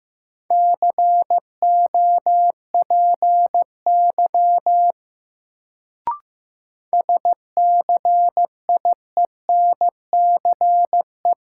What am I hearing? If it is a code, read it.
Morse code, heard as a keyed tone, about 15 wpm, 700 Hz. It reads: COPY SCIENCE
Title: Sentences from Top 1400 Words - New Word Drill Alt - 15wpm